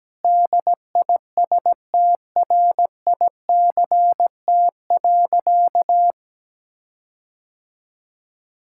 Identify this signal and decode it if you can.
Morse code, heard as a keyed tone, about 17 wpm, 700 Hz. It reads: DISTRICT.
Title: Sentences from Top 1600 Words 17wpm